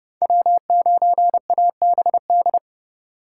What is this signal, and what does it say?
Morse code, heard as a keyed tone, about 40 wpm, 700 Hz. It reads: W9ABD